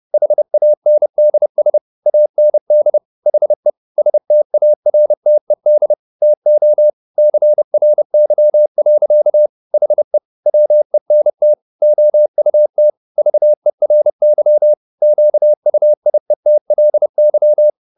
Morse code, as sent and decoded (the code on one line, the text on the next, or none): .... .- -. -.. ... / .- -. -.. / .... / ... - .- .-. - . -.. / - --- / -.-. .-. -.-- .-.-.- / .... / .-- . -. - / --- ..- - / ...- . .-. -.-- / --.- ..- .. - .-.. -.--
HANDS AND HE STARTED TO CRY. HE WENT OUT VERY QUIETLY